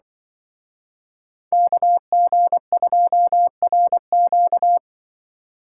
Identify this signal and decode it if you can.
Morse code, heard as a keyed tone, about 24 wpm, 700 Hz. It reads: KG2RQ
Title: Speed Racing - USA 2x2 Call Signs 20wpm